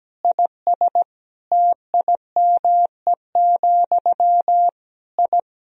Morse code, heard as a keyed tone, about 17 wpm, 700 Hz. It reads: IS TIME, I
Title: Sentences from Top 1000 Words - Review Part 1 - 17wpm